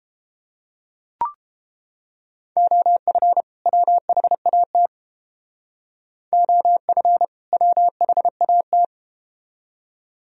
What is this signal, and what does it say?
Morse code, heard as a keyed tone, about 31 wpm, 700 Hz. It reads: OF WHAT OF WHAT